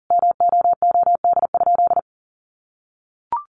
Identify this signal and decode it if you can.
Morse code, heard as a keyed tone, about 40 wpm, 700 Hz. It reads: MOOD?
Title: Sentences from Top 1100 Words 40wpm